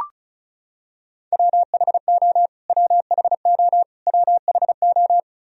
Morse code, heard as a keyed tone, about 36 wpm, 700 Hz. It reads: WHO WHO WHO
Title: Top 100 Words Repeated 35wpm